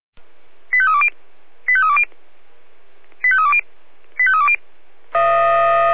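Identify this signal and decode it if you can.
Five-tone selective call (ZVEI), heard as quick runs of five tones, tones of about 70 ms. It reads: 85320 85320 85320 85320
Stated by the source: ZVEI five-tone call 89236 received on SDR.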